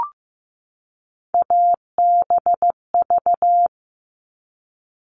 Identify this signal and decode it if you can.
Morse code, heard as a keyed tone, about 15 wpm, 700 Hz. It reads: ABV
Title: All Cut Numbers - 3-Digit 15wpm